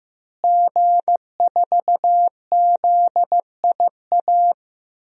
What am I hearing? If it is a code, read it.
Morse code, heard as a keyed tone, about 16 wpm, 700 Hz. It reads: G4ZIA
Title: DXCC-100 Call Signs - 5 Characters - Repeated 3x - 15wpm